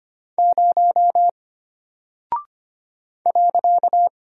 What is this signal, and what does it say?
Morse code, heard as a keyed tone, about 25 wpm, 700 Hz. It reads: 0 .
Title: Single Character (Letter:Number:Punctuation:Prosign) Alt 25wpm